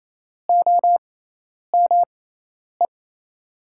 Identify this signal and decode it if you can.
Morse code, heard as a keyed tone, about 28 wpm, spaced as for 10 wpm, 700 Hz. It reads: OME
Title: 6x Character Spacing - Top 200 Words - 28wpm